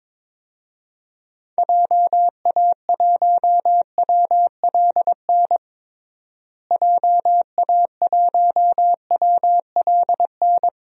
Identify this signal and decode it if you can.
Morse code, heard as a keyed tone, about 22 wpm, 700 Hz. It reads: JA1WLN JA1WLN
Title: DXCC-100 Call Signs - 6 Characters - Repeated 2x - 22wpm